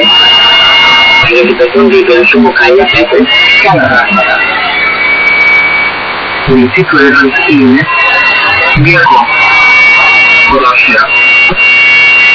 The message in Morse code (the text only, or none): none